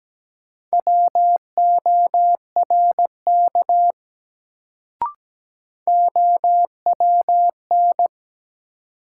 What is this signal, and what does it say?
Morse code, heard as a keyed tone, about 17 wpm, 700 Hz. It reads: WORK OWN
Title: Sets of 2 Words - New Word Drill - 17wpm